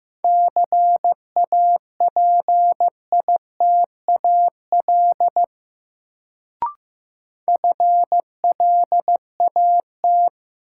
Morse code, heard as a keyed tone, about 15 wpm, 700 Hz. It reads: CAPITAL FLAT
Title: Sentences from Top 1000 Words - New Word Drill - 15wpm